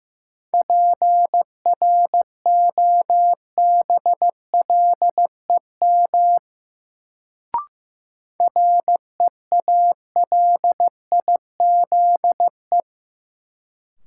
Morse code, heard as a keyed tone, about 15 wpm, 700 Hz. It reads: PROBLEM REALIZE
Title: Sentences from Top 1000 Words - New Word Drill - 15wpm